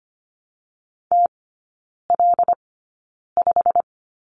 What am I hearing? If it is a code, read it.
Morse code, heard as a keyed tone, about 24 wpm, spaced as for 9 wpm, 700 Hz. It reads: TL5
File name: Lesson 15e - Number 5 - Three Characters 25wpm